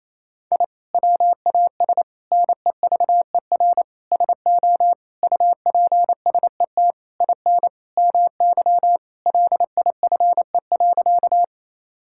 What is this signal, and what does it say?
Morse code, heard as a keyed tone, about 28 wpm, 700 Hz. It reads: I WAS NEVER SO UPSET IN MY LIFE.
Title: Sentences from Top 1000 Words - Review Part 5 - 28wpm